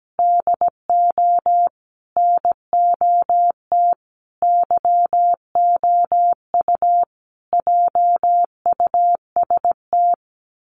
Morse code, heard as a keyed tone, about 17 wpm, 700 Hz. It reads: DO NOT YOU JUST
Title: Sets of 5 Words 17wpm